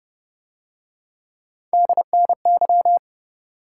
Morse code, heard as a keyed tone, about 30 wpm, 700 Hz. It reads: DNY